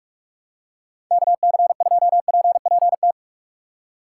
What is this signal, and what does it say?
Morse code, heard as a keyed tone, about 44 wpm, 700 Hz. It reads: KC2PPT
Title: USA Call Signs - 6 Characters - 6x Character Spacing - 45wpm